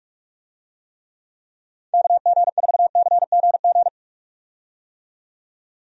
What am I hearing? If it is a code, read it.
Morse code, heard as a keyed tone, about 46 wpm, 700 Hz. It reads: KG4CGG
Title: USA Call Signs - 6 Characters - 45wpm